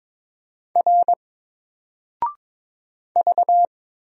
Morse code, heard as a keyed tone, about 22 wpm, 700 Hz. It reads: R V
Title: Single Character (Letter:Number:Punctuation:Prosign) 22wpm